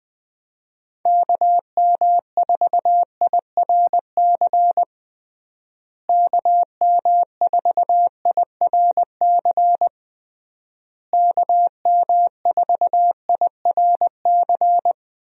Morse code, heard as a keyed tone, about 20 wpm, 700 Hz. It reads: KM4IRC KM4IRC KM4IRC
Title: USA Call Signs - 6 Characters - Repeated 3x - 20wpm